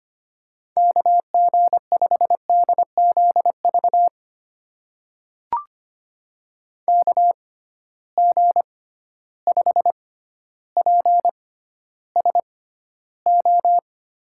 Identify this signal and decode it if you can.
Morse code, heard as a keyed tone, about 25 wpm, spaced as for 13 wpm, 700 Hz. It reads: KG5DZV KG5PSO
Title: USA Call Signs - 6 Characters - 6x Character Spacing - 25wpm